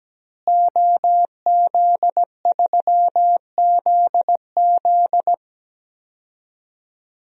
Morse code, heard as a keyed tone, about 17 wpm, 700 Hz. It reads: OZ3ZZ
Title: DXCC-100 Call Signs - 5 Characters - Repeated 3x - 17wpm